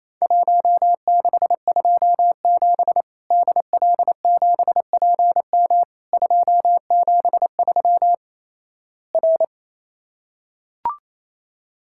Morse code, heard as a keyed tone, about 28 wpm, 700 Hz. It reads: 1627 DL7PM 273 R
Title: Worked All Europe DX Contest - QTC Exchange - 28wpm